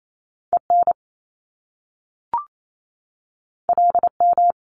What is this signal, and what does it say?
Morse code, heard as a keyed tone, about 28 wpm, 700 Hz. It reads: EN LM
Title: Top Two Letter Combinations in Words 28wpm